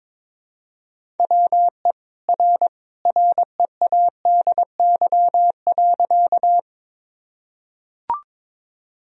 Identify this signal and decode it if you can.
Morse code, heard as a keyed tone, about 22 wpm, 700 Hz. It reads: WE R READY.
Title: Sentences from Top 400 Words and Abbreviations 22wpm